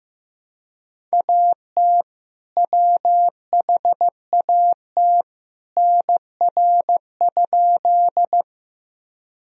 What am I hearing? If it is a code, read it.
Morse code, heard as a keyed tone, about 15 wpm, 700 Hz. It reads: AT WHAT NR?
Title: Sentences from Top 100 Words and Abbreviations 15wpm